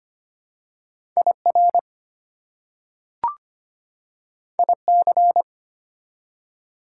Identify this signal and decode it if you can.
Morse code, heard as a keyed tone, about 25 wpm, 700 Hz. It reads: IR IC
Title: Prefixes and Suffixes 25wpm